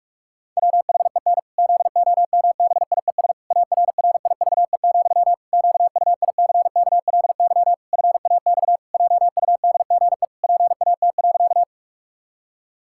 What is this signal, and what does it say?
Morse code, heard as a keyed tone, about 46 wpm, 700 Hz. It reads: WHEN ZOMBIES ARRIVE, QUICKLY FAX JUDGE PAT.
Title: Pangram Challenge v2 - 45wpm